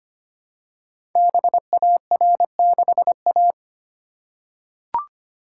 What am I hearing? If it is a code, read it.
Morse code, heard as a keyed tone, about 25 wpm, 700 Hz. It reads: BAR6A
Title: Code Groups - 5 Letter:Numbers - 25wpm